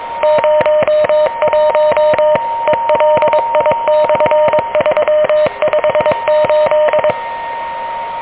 Morse code, heard as a keyed tone, about 23 wpm, 600 Hz, with noise. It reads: 01 ELI/358
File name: OK QRP 2023 / OK QRP 2023